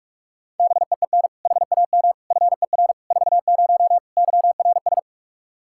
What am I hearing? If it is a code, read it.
Morse code, heard as a keyed tone, about 46 wpm, 700 Hz. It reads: BEEN HAM FER 40 YRS
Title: Practice QSO - Exchange 3 - More Info - 45wpm